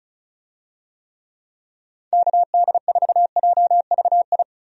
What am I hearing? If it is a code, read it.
Morse code, heard as a keyed tone, about 36 wpm, 700 Hz. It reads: KD4JVI
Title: USA Call Signs - 6 Characters - 35wpm